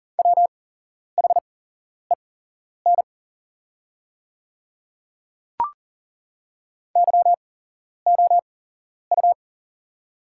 Morse code, heard as a keyed tone, about 40 wpm, spaced as for 11 wpm, 690 Hz. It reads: WHEN YOU
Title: 8x Character Spacing - Top 200 Words - 40wpm